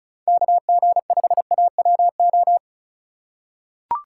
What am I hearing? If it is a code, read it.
Morse code, heard as a keyed tone, about 36 wpm, 700 Hz. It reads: KG5AWO
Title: USA Call Signs - 6 Characters - 3x Character Spacing - 35wpm